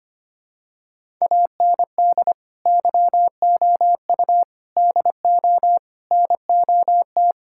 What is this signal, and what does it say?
Morse code, heard as a keyed tone, about 25 wpm, 700 Hz. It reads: AND YOU DO NOT
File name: practice / 25wpm / Sentences with 4-Word Phrases - 25wpm